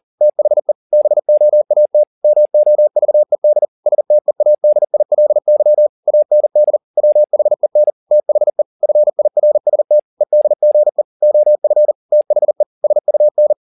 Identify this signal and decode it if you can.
Morse code, heard as a keyed tone, about 40 wpm, 600 Hz. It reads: THE BOAT MOVED STEADILY AND WHEN THE FIRST EDGE OF THE SUN